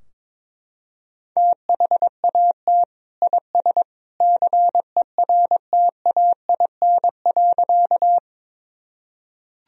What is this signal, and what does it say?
Morse code, heard as a keyed tone, about 22 wpm, 700 Hz. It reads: THAT IS CERTAIN.
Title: Sentences from Top 1000 Words - Review Part 2 - 22wpm